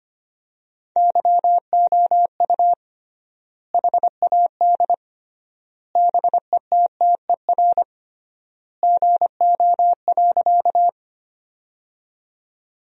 Morse code, heard as a keyed tone, about 25 wpm, 700 Hz. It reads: YOU HAD BETTER GO.
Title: 3x Word-Spacing - Sentences from Top 400 Words - 25wpm